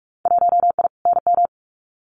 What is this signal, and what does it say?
Morse code, heard as a keyed tone, about 46 wpm, 700 Hz. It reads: 1I NM
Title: Winter Field Day Exchanges Encore 45wpm